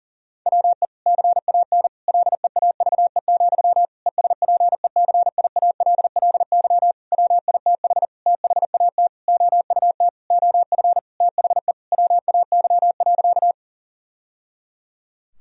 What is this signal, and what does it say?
Morse code, heard as a keyed tone, about 40 wpm, 700 Hz. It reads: WE CAN LEAVE, ESPECIALLY WITH THAT OUT OF THE WAY.
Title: Sentences from Top 1000 Words - Review Part 4 - 40wpm